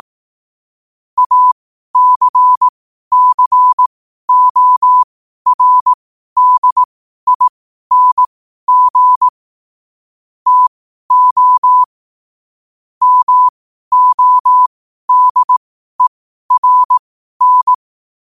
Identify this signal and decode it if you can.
Morse code, heard as a keyed tone, about 18 wpm, spaced as for 12 wpm, 1000 Hz. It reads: ACCORDING TO MODERN
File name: qotdfiles / 12wpm